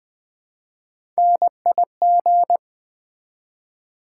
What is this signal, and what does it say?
Morse code, heard as a keyed tone, about 20 wpm, 700 Hz. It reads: NIG
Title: Build A Word - 1 Letter At A Time - 5-Letter Words - 20wpm